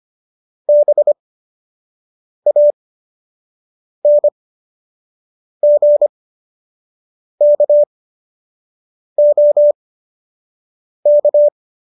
Morse code, heard as a keyed tone, about 25 wpm, spaced as for 6 wpm, 590 Hz. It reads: BANGKOK